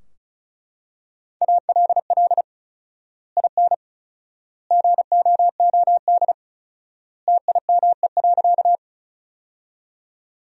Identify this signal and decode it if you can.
Morse code, heard as a keyed tone, about 34 wpm, 700 Hz. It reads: ALL IN GOOD TIME.
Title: 4x Word-Spacing - Sentences from Top 200 Words - 35wpm